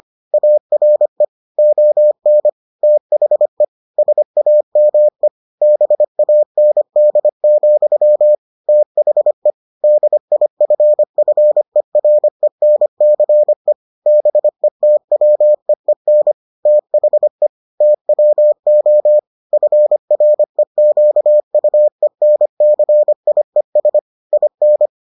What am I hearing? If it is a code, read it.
Morse code, heard as a keyed tone, about 25 wpm, 600 Hz. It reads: ARE ON THE SAME BAND, THE DIFFERENCE BETWEEN THE TWO FREQUENCIES IN